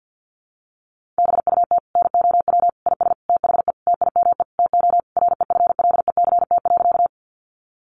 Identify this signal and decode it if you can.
Morse code, heard as a keyed tone, about 50 wpm, 700 Hz. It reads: BUT NOW IS THE TIME TO REFLECT.
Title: Sentences from Top 1500 Words 50wpm